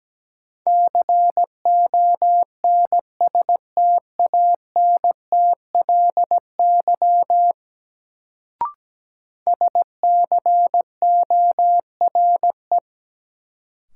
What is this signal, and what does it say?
Morse code, heard as a keyed tone, about 17 wpm, 700 Hz. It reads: CONSTANTLY SCORE
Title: Sentences from Top 1600 Words - New Word Drill Alt - 17wpm